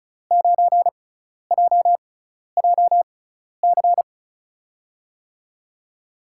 Morse code, heard as a keyed tone, about 36 wpm, spaced as for 12 wpm, 700 Hz. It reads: 9JJC
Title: USA Call Signs - 6x Character Spacing - 35wpm